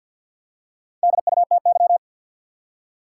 Morse code, heard as a keyed tone, about 52 wpm, 700 Hz. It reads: DUTY
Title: Four Letter Words - 50wpm